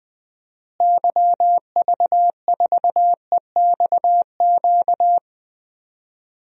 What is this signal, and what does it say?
Morse code, heard as a keyed tone, about 20 wpm, 700 Hz. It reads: YV4EXQ
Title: DXCC-100 Call Signs - 6 Characters - Repeated 2x - 20wpm